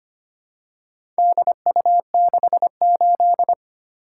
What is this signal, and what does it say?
Morse code, heard as a keyed tone, about 25 wpm, 700 Hz. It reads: DU68
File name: practice / 25wpm / Four Letter-Number Combinations - 25wpm